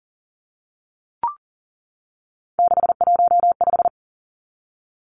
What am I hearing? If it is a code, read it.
Morse code, heard as a keyed tone, about 40 wpm, 700 Hz. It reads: 615
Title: Numbers - 3-Digit 40wpm